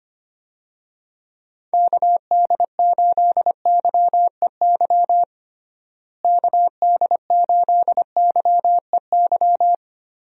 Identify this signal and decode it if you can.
Morse code, heard as a keyed tone, about 25 wpm, 700 Hz. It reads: KD8YEY KD8YEY
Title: USA Call Signs - 6 Characters - Repeated 2x - 25wpm